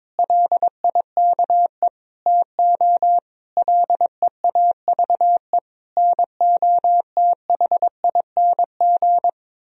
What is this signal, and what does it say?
Morse code, heard as a keyed tone, about 22 wpm, 700 Hz. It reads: LIKE TO LEAVE NOTHING
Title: Sentences from Top 1000 Words - Review Part 1 - 22wpm